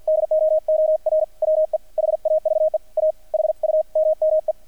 Morse code, heard as a keyed tone, about 50 wpm, 630 Hz, with radio noise. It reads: DOOR WE HAVE A HUMME
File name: archived / CRSnet